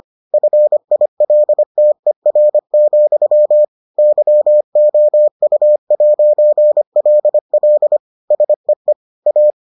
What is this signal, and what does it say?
Morse code, heard as a keyed tone, about 25 wpm, 600 Hz. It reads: FILTER, YOU'LL SEE A